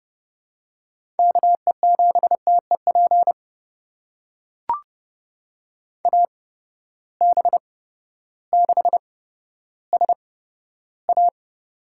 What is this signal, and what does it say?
Morse code, heard as a keyed tone, about 30 wpm, spaced as for 14 wpm, 700 Hz. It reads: KE7TEP AB6SA
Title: USA Call Signs - 8x Character Spacing - 30wpm